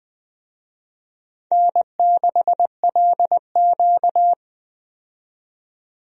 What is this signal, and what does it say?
Morse code, heard as a keyed tone, about 20 wpm, 700 Hz. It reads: N6LQ
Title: USA Call Signs - 4 Characters - 20wpm